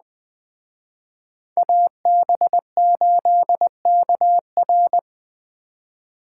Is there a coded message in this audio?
Yes: Morse code from 1.6 to 5.0 s, heard as a keyed tone, about 20 wpm, 700 Hz. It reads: AB8KR